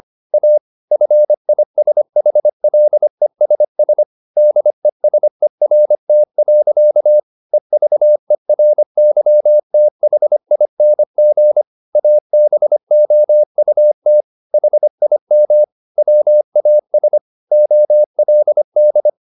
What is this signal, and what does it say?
Morse code, heard as a keyed tone, about 25 wpm, 600 Hz. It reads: A FISHLESS DESERT. EVERYTHING ABOUT HIM WAS OLD
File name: the old man and the sea part 1 25wpm